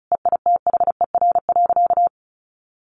Morse code, heard as a keyed tone, about 34 wpm, 700 Hz. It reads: EITHER.
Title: Sentences from Top 1200 Words 35wpm